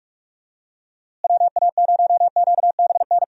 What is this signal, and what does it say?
Morse code, heard as a keyed tone, about 46 wpm, 700 Hz. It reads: WA0QBN
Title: USA Call Signs - 6 Characters - Repeated 3x - 45wpm